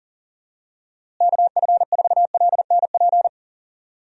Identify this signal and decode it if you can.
Morse code, heard as a keyed tone, about 40 wpm, 700 Hz. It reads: KF4LNP
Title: USA Call Signs - 6 Characters - Repeated 3x - 40wpm